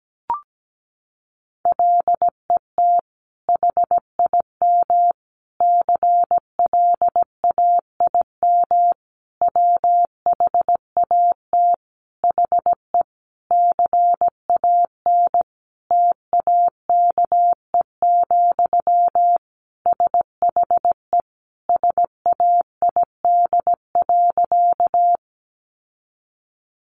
Morse code, heard as a keyed tone, about 17 wpm, 700 Hz. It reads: LET HIM CLAIM WHAT HE CAN TAKE, SHE SAID.
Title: Sentences from Top 1500 Words 17wpm